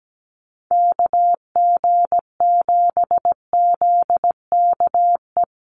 Morse code, heard as a keyed tone, about 17 wpm, 700 Hz. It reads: KG7ZKE
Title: USA Call Signs Repeated 3x 17wpm